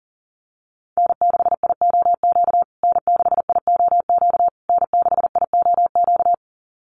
Morse code, heard as a keyed tone, about 37 wpm, 700 Hz. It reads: N6IOQ N6IOQ N6IOQ
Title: USA Call Signs Repeated 3x 40wpm